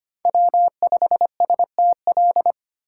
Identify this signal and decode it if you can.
Morse code, heard as a keyed tone, about 25 wpm, 700 Hz. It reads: W5STL
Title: USA Call Signs - 5 Characters - 2x Character Spacing - 25wpm